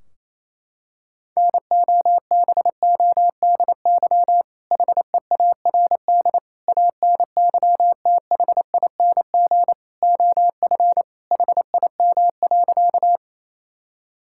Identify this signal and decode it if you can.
Morse code, heard as a keyed tone, about 28 wpm, 700 Hz. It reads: NOBODY HEARD ANYTHING OF HIM.
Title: Sentences from Top 1600 Words 28wpm